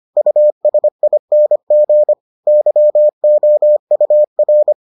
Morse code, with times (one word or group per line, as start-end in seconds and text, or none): 0.2-2.1 USING
2.5-4.7 YOUR